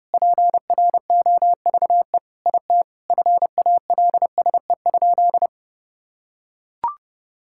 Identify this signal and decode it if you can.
Morse code, heard as a keyed tone, about 30 wpm, 700 Hz. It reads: PROVE IT FALSE?
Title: Sentences from Top 1600 Words 30wpm